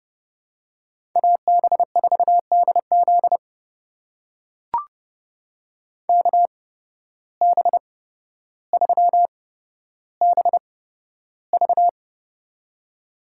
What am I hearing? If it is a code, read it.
Morse code, heard as a keyed tone, about 32 wpm, spaced as for 13 wpm, 700 Hz. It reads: AB4DZ KB3BV
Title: USA Call Signs - 5 Characters - 8x Character Spacing - 30wpm